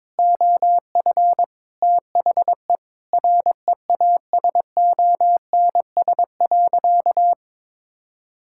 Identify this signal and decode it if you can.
Morse code, heard as a keyed tone, about 22 wpm, 700 Hz. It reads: OF THE REASONS.
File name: practice / 22wpm / Sentences with 4-Word Phrases - 22wpm